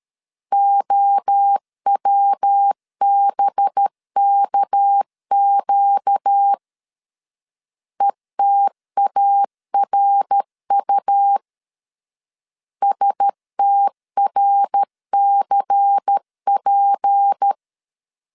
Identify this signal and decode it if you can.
Morse code, heard as a keyed tone, about 13 wpm, 800 Hz. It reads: OWBKQ ETARU STRCP
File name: LEZIONI / 6 PKX0 / 6 PK 12wpm 2006